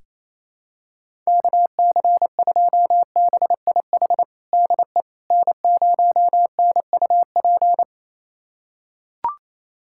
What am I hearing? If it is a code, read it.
Morse code, heard as a keyed tone, about 27 wpm, 700 Hz. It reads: KC2BIH DE N0NUP
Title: USA Callsign DE Callsign 28wpm